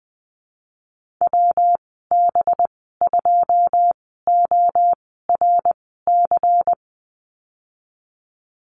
Morse code, heard as a keyed tone, about 20 wpm, spaced as for 15 wpm, 700 Hz. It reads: WB2ORC
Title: 186 Lesson 30g - Letter b - Call Signs 20wpm